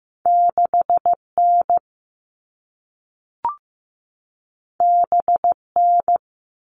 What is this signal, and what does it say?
Morse code, heard as a keyed tone, about 15 wpm, 700 Hz. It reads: 6N BN
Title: All Cut Numbers - 2-Digit 15wpm